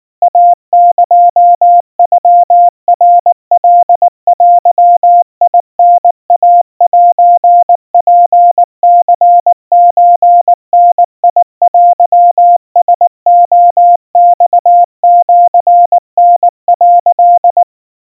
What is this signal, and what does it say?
Morse code (Japanese Wabun code), heard as a keyed tone, about 19 wpm, 700 Hz. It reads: イエノナカデタイセツニソダテラレマシタ」
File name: kawa / Morse sample / kamyu 70 cpm 700 Hz Crispy